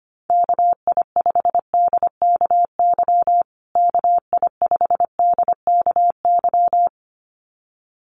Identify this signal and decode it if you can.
Morse code, heard as a keyed tone, about 25 wpm, 700 Hz. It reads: KI5DKY KI5DKY
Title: USA Call Signs Repeated 2x 25wpm